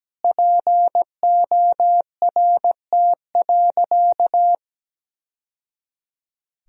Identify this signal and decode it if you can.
Morse code, heard as a keyed tone, about 17 wpm, 700 Hz. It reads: PORT.
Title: Sentences from Top 1000 Words - Review Part 1 - 17wpm